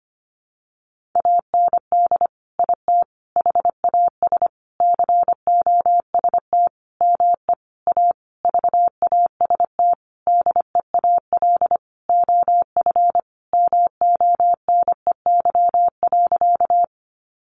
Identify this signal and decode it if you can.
Morse code, heard as a keyed tone, about 25 wpm, 700 Hz. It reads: AND IT HAS COST ME A VAST DEAL OF MONEY.
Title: Sentences from Top 1300 Words 25wpm